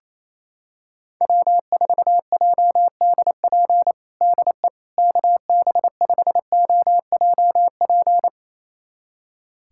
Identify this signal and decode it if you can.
Morse code, heard as a keyed tone, about 28 wpm, 700 Hz. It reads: W4JDP DE KB5OJP
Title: USA Callsign DE Callsign 28wpm